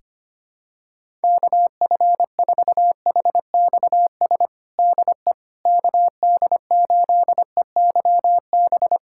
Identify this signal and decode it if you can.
Morse code, heard as a keyed tone, about 25 wpm, 700 Hz. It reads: KF4HXS DE KD8EYB
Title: USA Callsign DE Callsign - 6 Characters - 25wpm